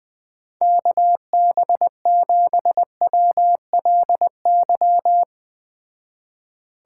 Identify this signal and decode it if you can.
Morse code, heard as a keyed tone, about 20 wpm, 700 Hz. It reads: KB7WLY